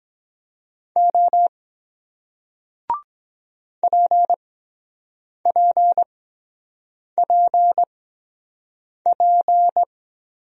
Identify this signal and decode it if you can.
Morse code, heard as a keyed tone, about 22 wpm, spaced as for 7 wpm, 700 Hz. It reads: O PPPP